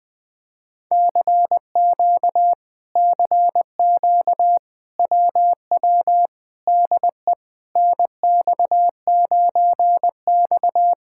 Morse code, heard as a keyed tone, about 20 wpm, 700 Hz. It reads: CQ CQ WW DE NX9X
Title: Speed-Racing Common QSO Phrases Encore - 20wpm